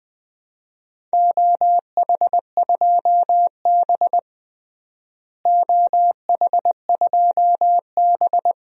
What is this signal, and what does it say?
Morse code, heard as a keyed tone, about 21 wpm, 700 Hz. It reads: OH2B OH2B
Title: DXCC-100 Call Signs - 4 Characters - Repeated 2x - 20wpm